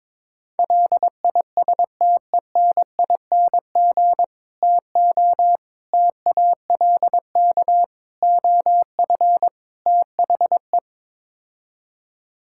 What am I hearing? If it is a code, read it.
Morse code, heard as a keyed tone, about 21 wpm, 700 Hz. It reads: LISTENING TO TALK OF THE
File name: practice / 22wpm / Build A Phrase - 5-Word Phrases - 22wpm